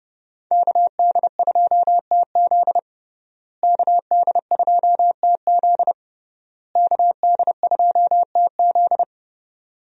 Morse code, heard as a keyed tone, about 32 wpm, 700 Hz. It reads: KD2TZ KD2TZ KD2TZ
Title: USA Call Signs - 5 Characters - Repeated 3x - 30wpm